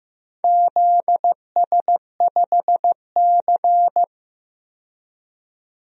Morse code, heard as a keyed tone, about 15 wpm, 700 Hz. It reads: ZS5C
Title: DXCC-100 Call Signs - 4 Characters - 15wpm